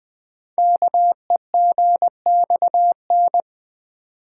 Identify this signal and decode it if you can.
Morse code, heard as a keyed tone, about 20 wpm, 700 Hz. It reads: KEGXN
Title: Code Groups - 5 Letters - 20wpm